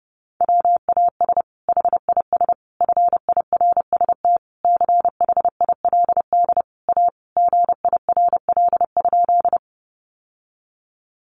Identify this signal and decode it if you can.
Morse code, heard as a keyed tone, about 30 wpm, 700 Hz. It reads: WAS HIS FIRST CHILD A GIRL?
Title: Sentences from Top 800 Words 30wpm